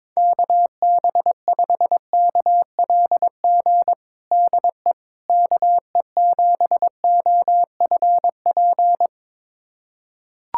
Morse code, heard as a keyed tone, about 22 wpm, 700 Hz. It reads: KB5KLG DE KE7OFP
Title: USA Callsign DE Callsign 22wpm